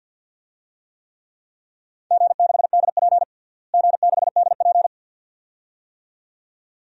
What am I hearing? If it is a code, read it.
Morse code, heard as a keyed tone, about 52 wpm, 700 Hz. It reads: G6DP G6DP
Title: DXCC-100 Call Signs - 4 Characters - Repeated 2x - 50wpm